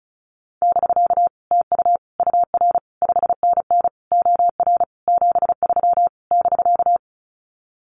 Morse code, heard as a keyed tone, about 34 wpm, 700 Hz. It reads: <BK> TU UR 5NN OR 73 <BK>
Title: Common QSO Phrases Encore 35wpm